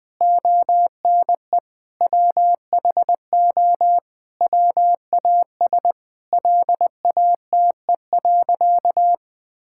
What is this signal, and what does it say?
Morse code, heard as a keyed tone, about 20 wpm, 700 Hz. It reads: ONE WHO WAS LATE.
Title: Sentences with 2-Word Phrases - 20wpm